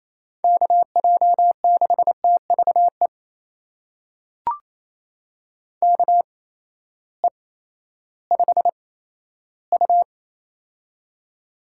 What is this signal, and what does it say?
Morse code, heard as a keyed tone, about 27 wpm, spaced as for 14 wpm, 700 Hz. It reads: KJ6TVE KE5U
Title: USA Call Signs - 8x Character Spacing - 28wpm